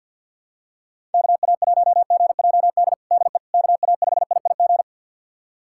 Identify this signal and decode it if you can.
Morse code, heard as a keyed tone, about 49 wpm, 690 Hz. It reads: KA1GJD DE KA5IIG
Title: USA Callsign DE Callsign - 6 Characters - 50wpm